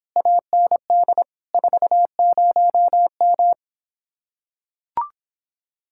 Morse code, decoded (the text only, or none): AND 40M